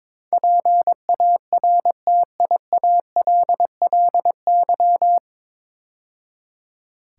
Morse code, heard as a keyed tone, about 22 wpm, 700 Hz. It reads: PARTIALLY
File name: 9 Letter Words - First 3 Letters Given - 22wpm